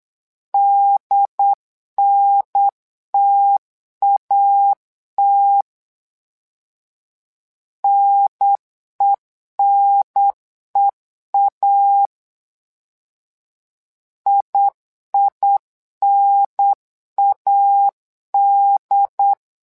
Morse code, heard as a keyed tone, about 8 wpm, 800 Hz. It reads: DNTAT NENEA IINAD
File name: LEZIONI / 2 UHDN / 2 UHDN 8wpm2232